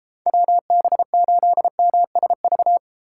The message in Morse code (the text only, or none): WB8MSV